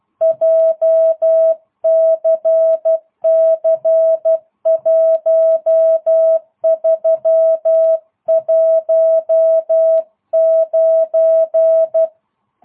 Morse code, heard as a keyed tone, about 12 wpm, 650 Hz. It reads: JCC 1319